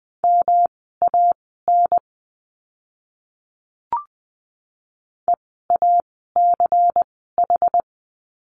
Morse code, heard as a keyed tone, about 20 wpm, spaced as for 14 wpm, 700 Hz. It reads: MAN EACH
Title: 192 Lesson 31f - Letter v - Words 20wpm